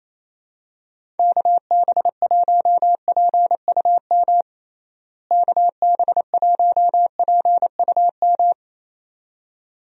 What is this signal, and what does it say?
Morse code, heard as a keyed tone, about 28 wpm, 700 Hz. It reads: KB1PUM KB1PUM
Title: USA Call Signs - 6 Characters - Repeated 2x - 28wpm